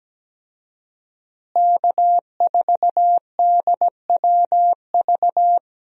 Morse code, heard as a keyed tone, about 17 wpm, 700 Hz. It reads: K4DWV
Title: USA Call Signs - 5 Characters - Repeated 2x - 17wpm